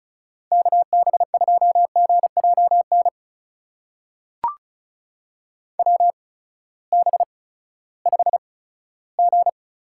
Morse code, heard as a keyed tone, about 34 wpm, spaced as for 17 wpm, 700 Hz. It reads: KB2GJN WB5G
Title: USA Call Signs - 6 Characters - 8x Character Spacing - 35wpm